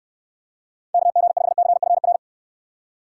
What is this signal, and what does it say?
Morse code, heard as a keyed tone, about 69 wpm, 700 Hz. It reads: KG5CFR